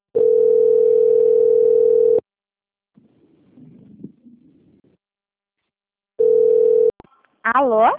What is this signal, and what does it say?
Ringback tone (North America), 440 + 480 Hz.
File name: out- - - - - .29017